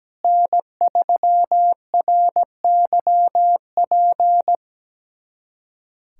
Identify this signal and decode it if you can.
Morse code, heard as a keyed tone, about 17 wpm, 700 Hz. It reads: N3RYP
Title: USA Callsign DE Callsign - 5 Characters - 17wpm